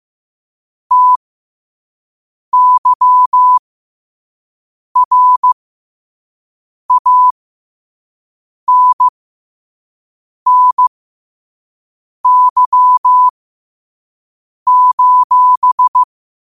Morse code, heard as a keyed tone, about 15 wpm, spaced as for 5 wpm, 1000 Hz. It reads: TYRANNY:
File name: qotdfiles / 5wpm